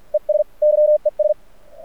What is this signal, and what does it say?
Morse code, heard as a keyed tone, about 44 wpm, 600 Hz. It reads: ES YES
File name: archived / CRSnet